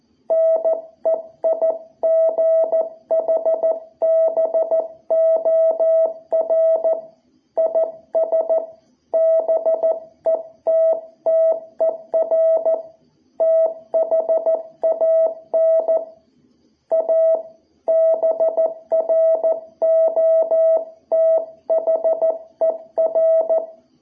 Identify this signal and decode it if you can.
Morse code, heard as a keyed tone, about 14 wpm, 660 Hz. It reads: NEIGHBOR IS BETTER THAN A BROTHER
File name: jh8cbh / 2021cw